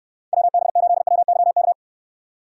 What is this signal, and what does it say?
Morse code, heard as a keyed tone, about 66 wpm, 700 Hz. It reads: WB8WYK